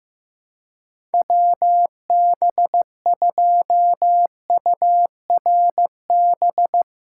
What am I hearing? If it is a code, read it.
Morse code, heard as a keyed tone, about 15 wpm, 700 Hz. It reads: WB2URB
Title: USA Callsign DE Callsign 15wpm